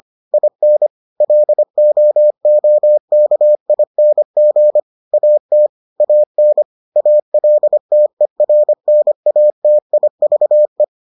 Morse code, heard as a keyed tone, about 25 wpm, 600 Hz. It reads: IN LOOKING AT AN ALTERNATIVE